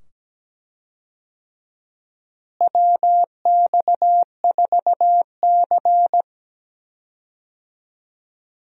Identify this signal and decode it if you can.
Morse code, heard as a keyed tone, about 17 wpm, 700 Hz. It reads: WX4C
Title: USA Call Signs - 4 Characters - Repeated 2x - 17wpm